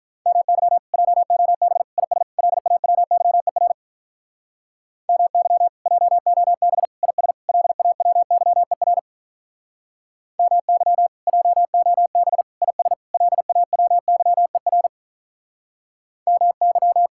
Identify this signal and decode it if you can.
Morse code, heard as a keyed tone, about 43 wpm, 700 Hz. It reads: MY JOB IS LAWYER MY JOB IS LAWYER MY JOB IS LAWYER MY
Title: Speed-Racing Common QSO Phrases - 35wpm